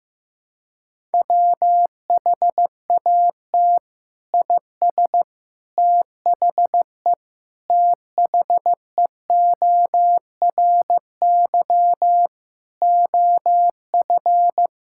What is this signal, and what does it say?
Morse code, heard as a keyed tone, about 15 wpm, 700 Hz. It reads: WHAT IS THE THEORY OF
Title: Sentences from Top 1000 Words - Review Part 6 - 15wpm